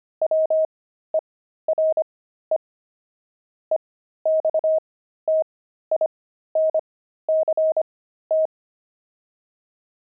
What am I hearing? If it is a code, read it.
Morse code, heard as a keyed tone, about 25 wpm, spaced as for 13 wpm, 630 Hz. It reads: WERE EXTINCT
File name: pangram-fs-13-cs-25-kidding